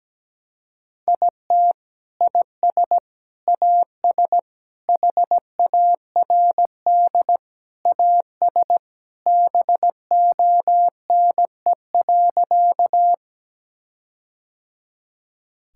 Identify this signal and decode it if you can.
Morse code, heard as a keyed tone, about 17 wpm, 700 Hz. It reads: IT IS AS HARD AS BONE.
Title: Sentences from Top 1600 Words 17wpm